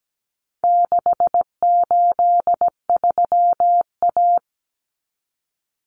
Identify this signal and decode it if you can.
Morse code, heard as a keyed tone, about 17 wpm, 700 Hz. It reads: 683A
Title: Common Cut Numbers - 4-Digit 17wpm